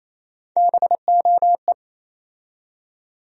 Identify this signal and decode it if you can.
Morse code, heard as a keyed tone, about 28 wpm, 700 Hz. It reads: BOE
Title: USA Call Sign Postfixes - 3 Letters 28wpm